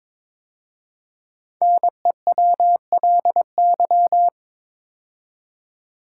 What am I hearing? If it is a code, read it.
Morse code, heard as a keyed tone, about 22 wpm, 700 Hz. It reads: NEWLY
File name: 5 Letter Words - First 3 Letters Given - 22wpm